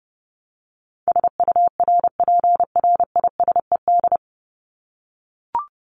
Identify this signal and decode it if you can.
Morse code, heard as a keyed tone, about 30 wpm, 700 Hz. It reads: SURPRISED
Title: Top 1501-1750 Words 30wpm